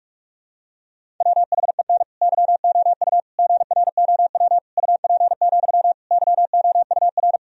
Morse code, heard as a keyed tone, about 42 wpm, 700 Hz. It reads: WHEN YOU GROW UP, YOUR
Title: Sentences from Top 1000 Words - Review Part 1 - 45wpm